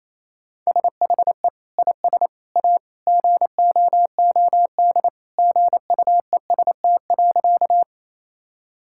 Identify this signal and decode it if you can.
Morse code, heard as a keyed tone, about 28 wpm, 700 Hz. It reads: SHE IS A GOOD GUEST.
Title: Sentences from Top 1600 Words 28wpm